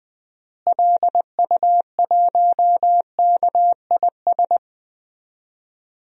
Morse code, heard as a keyed tone, about 20 wpm, 700 Hz. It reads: LU1KIS
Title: DXCC-100 Call Signs - 6 Characters - Repeated 2x - 20wpm